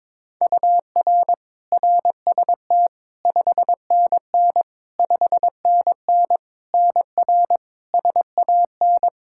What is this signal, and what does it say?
Morse code, heard as a keyed tone, about 21 wpm, 700 Hz. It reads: UR RST 5NN 5NN NR SAN
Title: Practice QSO - Exchange 1 - Beginning - 22wpm